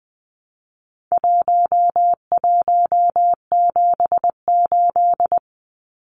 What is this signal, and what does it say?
Morse code, heard as a keyed tone, about 20 wpm, 700 Hz. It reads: 1178
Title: Numbers - 4-Digit 20wpm